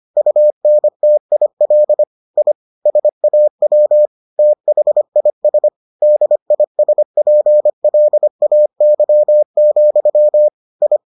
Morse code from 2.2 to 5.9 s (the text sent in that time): I SAW THIS